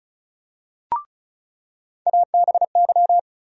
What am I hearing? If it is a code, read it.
Morse code, heard as a keyed tone, about 36 wpm, 700 Hz. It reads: ABY
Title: USA Call Sign Postfixes - 3 Letters 35wpm